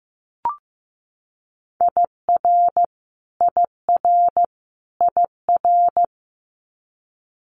Morse code, heard as a keyed tone, about 15 wpm, 700 Hz. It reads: IR IR IR
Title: Multi-letter Phonemes Repeated 3x 15wpm